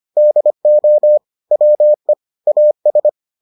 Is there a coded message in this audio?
Yes: Morse code from 0.2 to 3.1 s, heard as a keyed tone, about 25 wpm, 600 Hz. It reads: DO WE AS